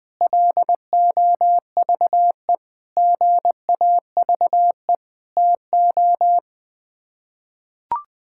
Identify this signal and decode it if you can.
Morse code, heard as a keyed tone, about 21 wpm, 700 Hz. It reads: LOVE GAVE TO
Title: Sets of 8 Words - 20wpm